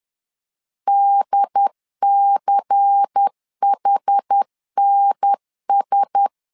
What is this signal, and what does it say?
Morse code, heard as a keyed tone, about 10 wpm, 800 Hz. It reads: DCHNS
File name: LEZIONI / 5 LBVC / 5 LBVC 10wpm 2349